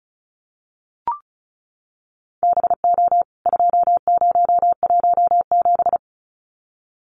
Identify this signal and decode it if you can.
Morse code, heard as a keyed tone, about 34 wpm, 700 Hz. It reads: BO 2017
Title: CWT Exchange 35wpm